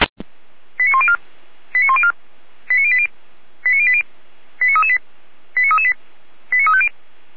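Five-tone selective call (ZVEI), heard as quick runs of five tones, tones of about 70 ms. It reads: 89104 89104 89080 89080 89338 89338 89350
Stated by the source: Five-tone ZVEI selective call received by SDR.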